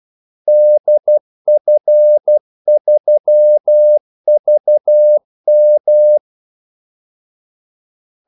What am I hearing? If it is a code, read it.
Morse code, heard as a keyed tone, about 12 wpm, 600 Hz. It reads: DF3VM
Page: 2 ekstra ordmellemrum mellem kaldesignalerne